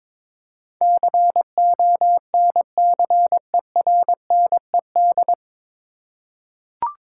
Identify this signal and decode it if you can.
Morse code, heard as a keyed tone, about 22 wpm, 700 Hz. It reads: CONCERNED
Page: Morse Code Ninja practice 22wpm